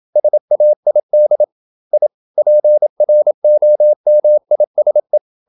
Morse code, heard as a keyed tone, about 28 wpm, 600 Hz. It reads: SAID I PROMISE
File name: Never 0027wpm